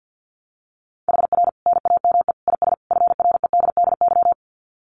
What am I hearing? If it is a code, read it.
Morse code, heard as a keyed tone, about 50 wpm, 700 Hz. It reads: HR NAME IS FREDDY
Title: Common QSO Phrases 50wpm